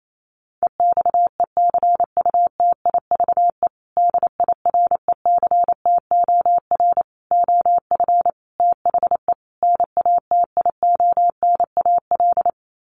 Morse code, heard as a keyed tone, about 28 wpm, 700 Hz. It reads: EXECUTIVE DIRECTOR OF THE NATIONAL
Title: Sets of 5 Words Encore2 28wpm